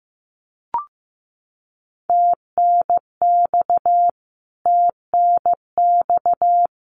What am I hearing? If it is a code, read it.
Morse code, heard as a keyed tone, about 15 wpm, 700 Hz. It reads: TNX TNX
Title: Abbreviations Repeated 3x - 15wpm